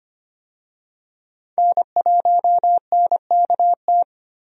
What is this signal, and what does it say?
Morse code, heard as a keyed tone, about 25 wpm, 690 Hz. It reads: N1NKT
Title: USA Call Signs - 5 Characters - 25wpm